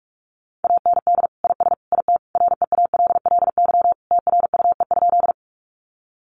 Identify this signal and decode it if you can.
Morse code, heard as a keyed tone, about 44 wpm, 700 Hz. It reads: AND IS IT REALLY TRUE?
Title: Sentences from Top 600 Words 45wpm